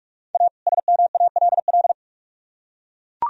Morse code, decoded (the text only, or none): A SMALL